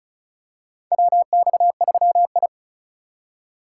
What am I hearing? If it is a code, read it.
Morse code, heard as a keyed tone, about 36 wpm, 700 Hz. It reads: WX3I
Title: USA Call Signs - 3x Character Spacing - 35wpm